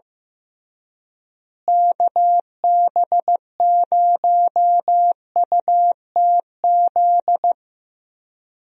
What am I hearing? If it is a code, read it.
Morse code, heard as a keyed tone, about 15 wpm, 700 Hz. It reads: KB0UTZ
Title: USA Call Signs - 6 Characters - Repeated 3x - 15wpm